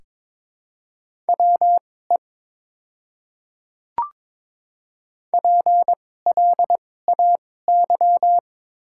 Morse code, heard as a keyed tone, about 22 wpm, spaced as for 16 wpm, 690 Hz. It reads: WE PLAY
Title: 2x Character Spacing - Top 500 Words - 22wpm